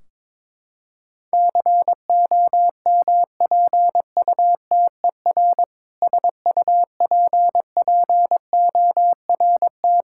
Morse code, heard as a keyed tone, about 22 wpm, 700 Hz. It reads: COMPUTER SUPPORT
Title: Practice QSO - Exchange 3 - More Info - 22wpm